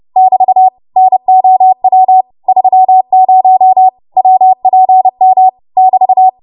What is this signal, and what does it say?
Morse code, heard as a keyed tone, about 30 wpm, 750 Hz. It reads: <BT> NOW 30 WPM <BT>
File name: Morse / 30WPM